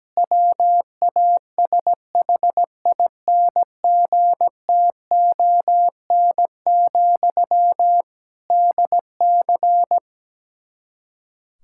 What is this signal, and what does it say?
Morse code, heard as a keyed tone, about 17 wpm, 700 Hz. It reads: WASHINGTON, DC
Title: Practice QSO - Exchange 1 - Beginning - 17wpm